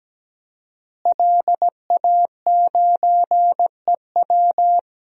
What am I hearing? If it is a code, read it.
Morse code, heard as a keyed tone, about 17 wpm, 700 Hz. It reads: LA9EW
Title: DXCC-100 Call Signs - 5 Characters - Repeated 2x - 17wpm